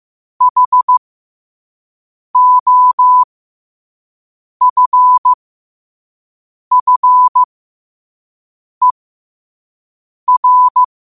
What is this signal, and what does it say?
Morse code, heard as a keyed tone, about 15 wpm, spaced as for 5 wpm, 1000 Hz. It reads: HOFFER
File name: qotdfiles / 5wpm